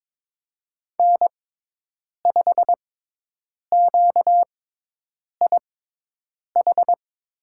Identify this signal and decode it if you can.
Morse code, heard as a keyed tone, about 22 wpm, spaced as for 8 wpm, 700 Hz. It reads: N5QIH
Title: USA Call Signs - 5 Characters - 6x Character Spacing - 22wpm